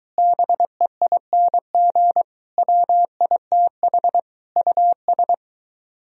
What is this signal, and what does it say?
Morse code, heard as a keyed tone, about 23 wpm, 700 Hz. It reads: BEING WITH US